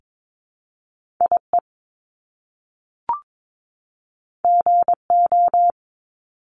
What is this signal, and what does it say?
Morse code, heard as a keyed tone, about 22 wpm, 700 Hz. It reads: IE GO
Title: Top Two Letter Combinations in Words 22wpm